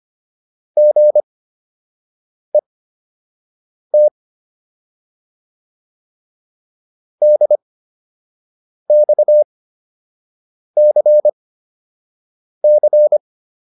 Morse code, heard as a keyed tone, about 25 wpm, spaced as for 6 wpm, 600 Hz. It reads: GET DXCC